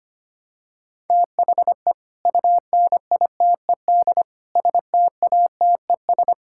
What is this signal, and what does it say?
Morse code, heard as a keyed tone, about 25 wpm, 700 Hz. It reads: THE UNITED STATES
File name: practice / 25wpm / Sentences with 4-Word Phrases - 25wpm